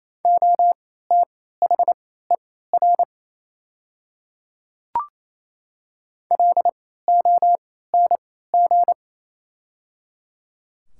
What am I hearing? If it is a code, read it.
Morse code, heard as a keyed tone, about 27 wpm, spaced as for 16 wpm, 700 Hz. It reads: OTHER LONG
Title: 3x Character Spacing - Top 100 Words - 28wpm